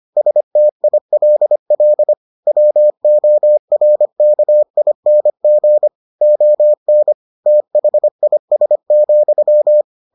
Morse code, heard as a keyed tone, about 25 wpm, 600 Hz. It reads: STILL WORKING ON THIS,